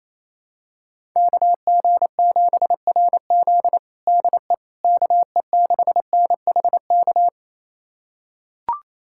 Morse code, heard as a keyed tone, about 28 wpm, 700 Hz. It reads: KG7RZ DE KE6NHK
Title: USA Callsign DE Callsign 28wpm